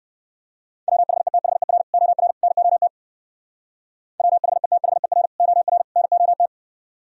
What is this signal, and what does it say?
Morse code, heard as a keyed tone, about 59 wpm, 700 Hz. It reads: WHETHER OR NOT WHETHER OR NOT